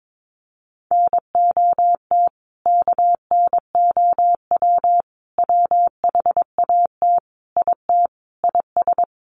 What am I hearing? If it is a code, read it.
Morse code, heard as a keyed tone, about 22 wpm, 700 Hz. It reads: NOT KNOW WHAT IT IS
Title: Sets of 5 Words Encore 22wpm